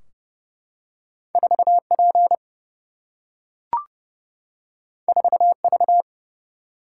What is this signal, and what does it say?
Morse code, heard as a keyed tone, about 30 wpm, 700 Hz. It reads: 4P 4V
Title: Single-Letter Single-Number 30wpm